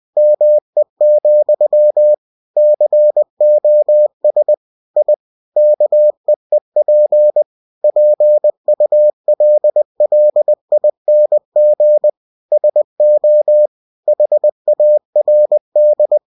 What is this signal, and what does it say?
Morse code, heard as a keyed tone, about 20 wpm, 600 Hz. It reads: ME, COS I KEEP PULLING SO HARD
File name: cw-books / winnie-the-pooh / winnie-20wpm-page-0076